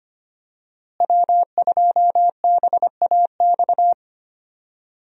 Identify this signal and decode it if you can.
Morse code, heard as a keyed tone, about 25 wpm, 700 Hz. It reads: W2BAX
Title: USA Call Signs - 5 Characters - Repeated 2x - 25wpm